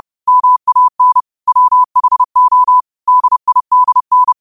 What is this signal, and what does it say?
Morse code, heard as a keyed tone, about 30 wpm, 1000 Hz. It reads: MAN WHO DIDN
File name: qotdfiles / 30wpm